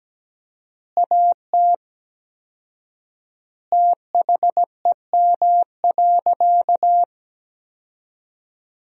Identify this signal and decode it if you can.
Morse code, heard as a keyed tone, about 17 wpm, 700 Hz. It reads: AT THEM.
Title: 4x Word-Spacing - Sentences from Top 100 Words - 17wpm